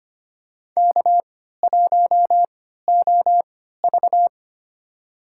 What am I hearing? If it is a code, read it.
Morse code, heard as a keyed tone, about 25 wpm, spaced as for 14 wpm, 700 Hz. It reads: K1OV
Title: USA Call Signs - 4 Characters - 3x Character Spacing - 25wpm